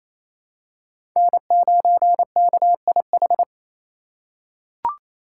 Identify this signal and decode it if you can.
Morse code, heard as a keyed tone, about 28 wpm, 700 Hz. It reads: N9KIH
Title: USA Call Signs - 3x Character Spacing - 28wpm